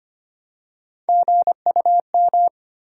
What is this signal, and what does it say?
Morse code, heard as a keyed tone, about 25 wpm, 700 Hz. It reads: GUM
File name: practice / 25wpm / Easy Words with Hints - 25wpm